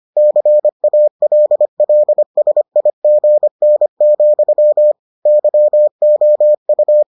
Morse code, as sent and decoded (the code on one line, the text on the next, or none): -.-. .- .-.. .-.. ... .. --. -. --..-- / -.-- --- ..-
CALLSIGN, YOU